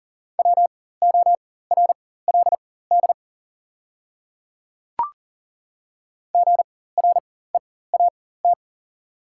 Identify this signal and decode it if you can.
Morse code, heard as a keyed tone, about 40 wpm, spaced as for 19 wpm, 690 Hz. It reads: WORLD GREAT
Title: 4x Character Spacing - Top 500 Words - 40wpm